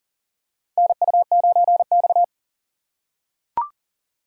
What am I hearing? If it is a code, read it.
Morse code, heard as a keyed tone, about 40 wpm, 700 Hz. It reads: NU9X
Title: USA Call Signs - 4 Characters - 2x Character Spacing - 40wpm